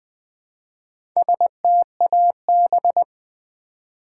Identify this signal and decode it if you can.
Morse code, heard as a keyed tone, about 20 wpm, 700 Hz. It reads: STAB
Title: Build A Word - 1 Letter At A Time - 4-Letter Words - 20wpm